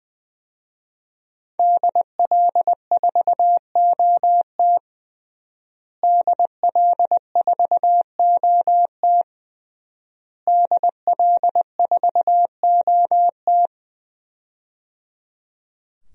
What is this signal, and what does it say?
Morse code, heard as a keyed tone, about 21 wpm, 700 Hz. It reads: DL4OT DL4OT DL4OT
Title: USA Call Signs - 5 Characters - Repeated 3x - 20wpm